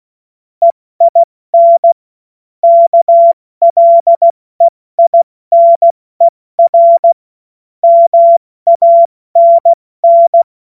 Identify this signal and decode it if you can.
Morse code, heard as a keyed tone, about 16 wpm, spaced as for 14 wpm, 680 Hz. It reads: EIN KLEINER MANN
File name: Morsekurs / Witze / Witz 22. Februar 2022 70 F80